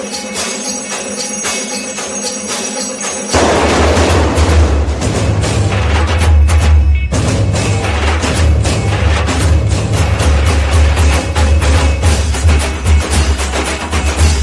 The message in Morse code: none